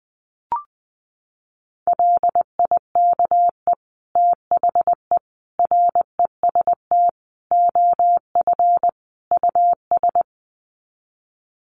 Morse code, heard as a keyed tone, about 20 wpm, 700 Hz. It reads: LIKE THE REST OF US
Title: Sets of 5 Words Encore2 20wpm